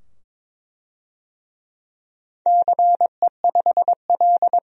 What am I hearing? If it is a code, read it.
Morse code, heard as a keyed tone, about 23 wpm, 700 Hz. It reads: CE5L
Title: DXCC-100 Call Signs - 4 Characters - 22wpm